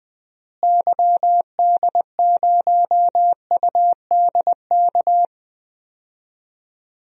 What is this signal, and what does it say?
Morse code, heard as a keyed tone, about 20 wpm, 700 Hz. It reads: YD0UDK
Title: DXCC-100 Call Signs - 6 Characters - Repeated 3x - 20wpm